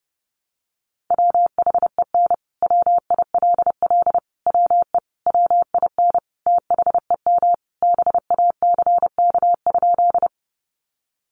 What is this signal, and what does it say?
Morse code, heard as a keyed tone, about 30 wpm, 700 Hz. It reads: WHEN WILL WE WIN THEM BACK?
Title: Sentences from Top 1500 Words 30wpm